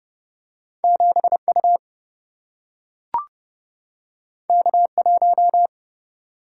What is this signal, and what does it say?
Morse code, heard as a keyed tone, about 30 wpm, 700 Hz. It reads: 7 U K 1
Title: Single-Letter Single-Number 30wpm